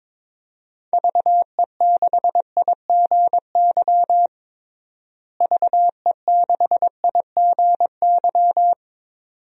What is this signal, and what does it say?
Morse code, heard as a keyed tone, about 22 wpm, 700 Hz. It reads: VE6IGY VE6IGY
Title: DXCC-100 Call Signs - 6 Characters - Repeated 2x - 22wpm